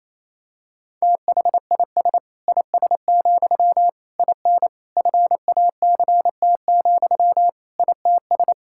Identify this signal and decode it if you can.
Morse code, heard as a keyed tone, about 27 wpm, 700 Hz. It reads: THIS IS, IN FACT, ITS
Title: Sentences from Top 1000 Words - Review Part 5 - 28wpm